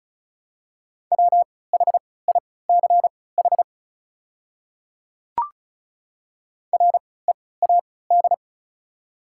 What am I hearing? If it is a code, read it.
Morse code, heard as a keyed tone, about 34 wpm, spaced as for 20 wpm, 690 Hz. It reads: WHICH READ
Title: 3x Character Spacing - Top 500 Words - 35wpm